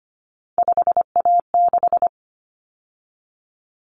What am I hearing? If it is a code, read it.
Morse code, heard as a keyed tone, about 25 wpm, 700 Hz. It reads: 5A6
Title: Common Cut Numbers - 3-Digit 25wpm